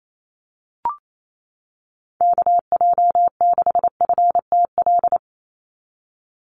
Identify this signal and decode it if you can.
Morse code, heard as a keyed tone, about 28 wpm, 700 Hz. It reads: KJ6FTL
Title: USA Call Signs 28wpm